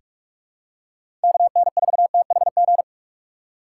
Morse code, heard as a keyed tone, about 46 wpm, 700 Hz. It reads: KN4THG
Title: USA Call Signs - 6x Character Spacing - 45wpm